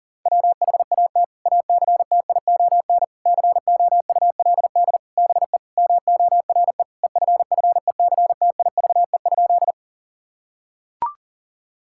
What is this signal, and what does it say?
Morse code, heard as a keyed tone, about 40 wpm, 700 Hz. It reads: WHAT ACTION COULD BE MORE EFFECTIVE?
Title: Sentences from Top 1000 Words - Review Part 6 - 40wpm